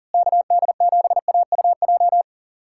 Morse code, heard as a keyed tone, about 40 wpm, 700 Hz. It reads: KD7AUJ
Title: USA Callsign DE Callsign - 6 Characters - 40wpm